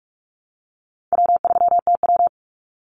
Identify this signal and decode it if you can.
Morse code, heard as a keyed tone, about 44 wpm, 700 Hz. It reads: W3TW
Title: Easy Call Signs Repeated 3x - 45wpm